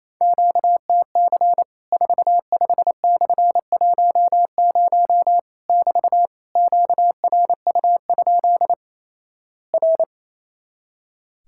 Morse code, heard as a keyed tone, about 28 wpm, 700 Hz. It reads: QTC 45/10 <BT> QRU? R
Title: Worked All Europe DX Contest - QTC Exchange - 28wpm